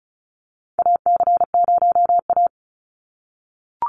Morse code, heard as a keyed tone, about 34 wpm, 700 Hz. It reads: AC0A
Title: NA SOTA Chaser Callsigns 35wpm